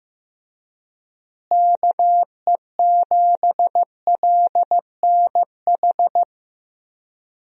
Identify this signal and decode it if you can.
Morse code, heard as a keyed tone, about 16 wpm, 700 Hz. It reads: KE7LNH
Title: USA Call Signs - 6 Characters - Repeated 3x - 15wpm